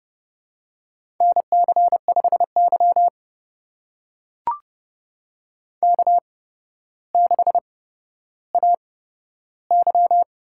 Morse code, heard as a keyed tone, about 30 wpm, spaced as for 13 wpm, 700 Hz. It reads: NC5Y K6AY
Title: USA Call Signs - 4 Characters - 8x Character Spacing - 30wpm